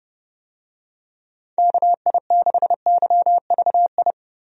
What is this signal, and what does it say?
Morse code, heard as a keyed tone, about 30 wpm, 700 Hz. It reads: KI6YVI
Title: USA Call Signs - 6 Characters - Repeated 2x - 30wpm